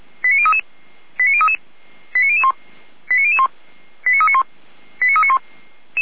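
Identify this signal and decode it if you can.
Five-tone selective call (ZVEI), heard as quick runs of five tones, tones of about 70 ms. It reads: 89033 89033 89001 89001 89391 89391
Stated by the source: ZVEI five-tone selective call 89503 received by SDR.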